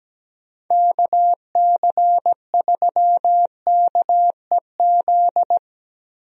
Morse code, heard as a keyed tone, about 17 wpm, 700 Hz. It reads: KC3KEZ
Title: USA Call Signs - 6 Characters - Repeated 3x - 17wpm